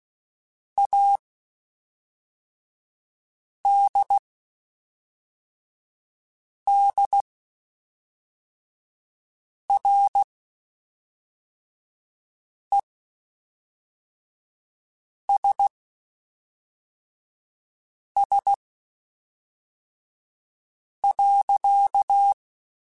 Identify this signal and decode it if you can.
Morse code, heard as a keyed tone, about 16 wpm, spaced as for 3 wpm, 800 Hz. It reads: ADDRESS.
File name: Spaced / Chuckles Spaced-mp3s / Temptation sp @5